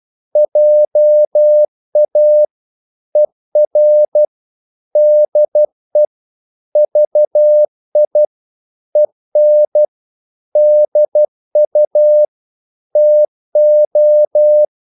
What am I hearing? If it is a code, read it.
Morse code, heard as a keyed tone, about 12 wpm, 600 Hz. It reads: JA ER DE VI EN DU TO
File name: Korte ord DK 12wpm